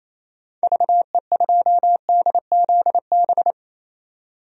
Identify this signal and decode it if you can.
Morse code, heard as a keyed tone, about 28 wpm, 700 Hz. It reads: VE2DZB
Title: DXCC-100 Call Signs - 6 Characters - 28wpm